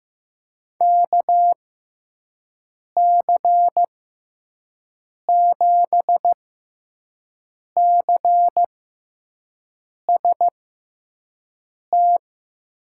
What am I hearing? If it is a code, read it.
Morse code, heard as a keyed tone, about 15 wpm, spaced as for 5 wpm, 700 Hz. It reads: KC7CST
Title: USA Call Signs - 6 Characters - 6x Character Spacing - 15wpm